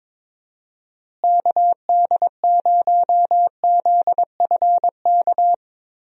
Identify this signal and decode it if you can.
Morse code, heard as a keyed tone, about 22 wpm, 700 Hz. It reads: KD0ZFK
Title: USA Call Signs - 6 Characters - Repeated 3x - 22wpm